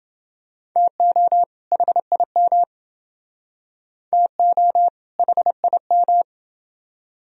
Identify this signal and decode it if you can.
Morse code, heard as a keyed tone, about 29 wpm, 700 Hz. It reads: TO HIM TO HIM